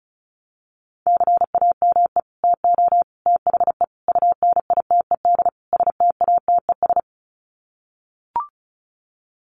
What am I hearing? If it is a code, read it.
Morse code, heard as a keyed tone, about 34 wpm, 700 Hz. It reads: CAME TO THE UNITED STATES
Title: Sets of 5 Words Encore2 35wpm